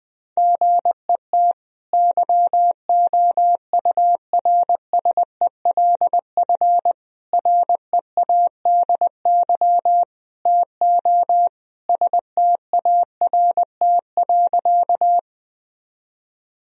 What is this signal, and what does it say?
Morse code, heard as a keyed tone, about 21 wpm, 700 Hz. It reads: GET YOURSELF READY TO START.
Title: Sentences from Top 1000 Words - Review Part 1 - 20wpm